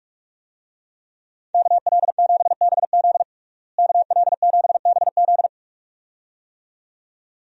Morse code, heard as a keyed tone, about 47 wpm, 700 Hz. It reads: KL7BZ KL7BZ
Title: DXCC-100 Call Signs - 5 Characters - Repeated 2x - 45wpm